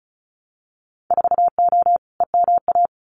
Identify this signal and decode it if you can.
Morse code, heard as a keyed tone, about 36 wpm, 700 Hz. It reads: 4O EMA
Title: Winter Field Day Exchanges Encore 35wpm